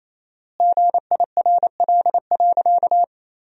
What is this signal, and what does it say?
Morse code, heard as a keyed tone, about 28 wpm, 700 Hz. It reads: GIRL.
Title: 4x Word-Spacing - Sentences from Top 400 Words - 28wpm